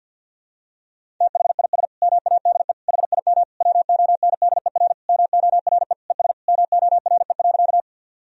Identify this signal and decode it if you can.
Morse code, heard as a keyed tone, about 52 wpm, 700 Hz. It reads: THIS MADE HIM WONDER MORE ES MORE.
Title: Sentences from Top 500 Words and Abbreviations 50wpm